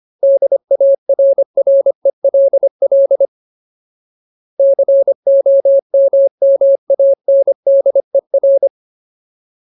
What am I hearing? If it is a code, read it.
Morse code, heard as a keyed tone, about 25 wpm, 550 Hz. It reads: DARRELL COMMANDER
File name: Ultimate Word List / 25WPM Ultimate